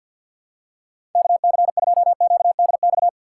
Morse code, heard as a keyed tone, about 52 wpm, 700 Hz. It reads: KC2QDX
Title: USA Call Signs - 6 Characters - 6x Character Spacing - 50wpm